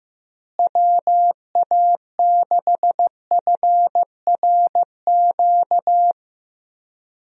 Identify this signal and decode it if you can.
Morse code, heard as a keyed tone, about 16 wpm, 700 Hz. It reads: WA6FRQ
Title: USA Call Signs - 6 Characters - Repeated 2x - 15wpm